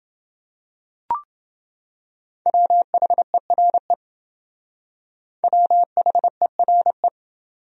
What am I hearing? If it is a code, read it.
Morse code, heard as a keyed tone, about 28 wpm, 700 Hz. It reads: WHERE WHERE